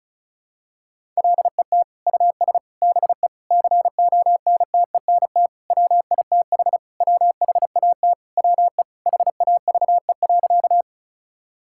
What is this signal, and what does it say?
Morse code, heard as a keyed tone, about 34 wpm, 700 Hz. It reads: LET US BE CONTENT WITH WHAT WE HAVE.
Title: Sentences from Top 1000 Words - Review Part 6 - 35wpm